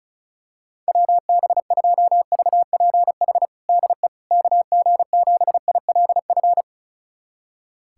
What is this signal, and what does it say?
Morse code, heard as a keyed tone, about 36 wpm, 700 Hz. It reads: WB2VPH DE KG7ILF
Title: USA Callsign DE Callsign 35wpm